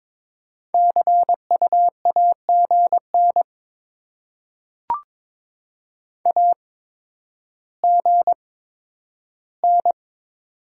Morse code, heard as a keyed tone, about 22 wpm, spaced as for 11 wpm, 700 Hz. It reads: CU AGN AGN
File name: Top 20 QSO Elements - 8x Character Spacing - 22wpm